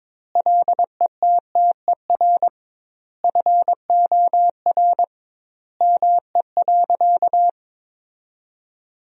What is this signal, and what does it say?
Morse code, heard as a keyed tone, about 22 wpm, 700 Hz. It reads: LETTER FOR ME.
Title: 2x Word-Spacing - Sentences from Top 300 Words - 22wpm